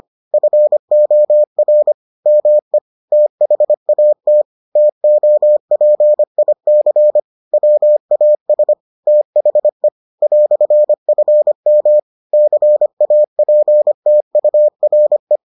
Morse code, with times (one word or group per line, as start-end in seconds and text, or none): 0.3-1.9 FOR
2.3-2.8 ME
3.1-4.4 THAT
4.8-7.2 TOPIC
7.5-8.7 WAS
9.1-9.9 THE
10.2-12.0 "FM
12.3-15.4 CAPTURE